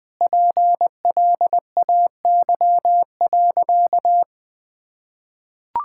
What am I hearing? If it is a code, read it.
Morse code, heard as a keyed tone, about 21 wpm, 700 Hz. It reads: PLAY.
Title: Sentences from Top 1000 Words - Review Part 6 - 20wpm